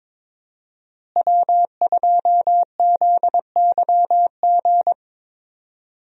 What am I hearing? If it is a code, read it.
Morse code, heard as a keyed tone, about 22 wpm, 700 Hz. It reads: W2ZYG
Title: USA Call Signs - 5 Characters - Repeated 2x - 22wpm